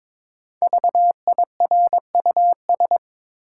Morse code, heard as a keyed tone, about 22 wpm, 700 Hz. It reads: VIRUS
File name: 5 Letter Words - First 3 Letters Given - 22wpm